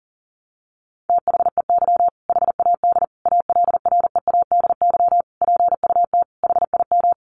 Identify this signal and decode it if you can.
Morse code, heard as a keyed tone, about 40 wpm, 700 Hz. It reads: THEY HAD ALREADY PUT HIM
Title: Sentences from Top 1200 Words 40wpm